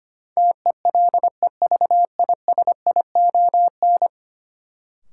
Morse code, heard as a keyed tone, about 25 wpm, 700 Hz. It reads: TELEVISION
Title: Sentences from Top 1000 Words - Review Part 4 - New Word Drill - 25wpm